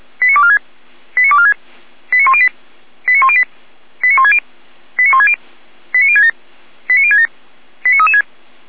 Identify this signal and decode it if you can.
Five-tone selective call (ZVEI), heard as quick runs of five tones, tones of about 70 ms. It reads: 89246 89246 89108 89108 89160 89160 89067 89067 89336